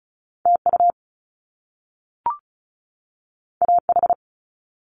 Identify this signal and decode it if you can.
Morse code, heard as a keyed tone, about 36 wpm, 700 Hz. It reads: TU AH
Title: Top Two Letter Combinations in Words 35wpm